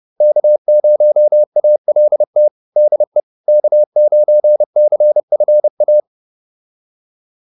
Morse code, heard as a keyed tone, about 30 wpm, 600 Hz. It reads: K0ALT DE K9CFA
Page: advanced-practice-files